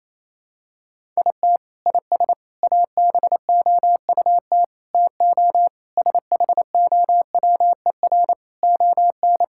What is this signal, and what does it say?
Morse code, heard as a keyed tone, about 28 wpm, 700 Hz. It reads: IT IS ABOUT TO SHOWER ON
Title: Sentences from Top 1600 Words 28wpm